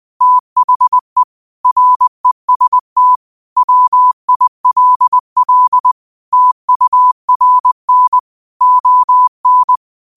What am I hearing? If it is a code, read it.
Morse code, heard as a keyed tone, about 20 wpm, 1000 Hz. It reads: THE REST WILL TURN ON
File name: qotdfiles / 20wpm